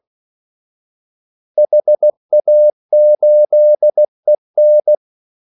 Morse code, heard as a keyed tone, about 16 wpm, 600 Hz. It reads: HA8EN
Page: Kaldesignaler 16 wpm